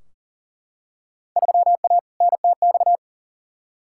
Morse code, heard as a keyed tone, about 40 wpm, 700 Hz. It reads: 3A NTX
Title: Field Day Exchanges - 40wpm